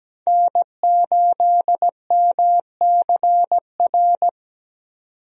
Morse code, heard as a keyed tone, about 17 wpm, 700 Hz. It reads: N8MCR
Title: USA Call Signs - 5 Characters - 17wpm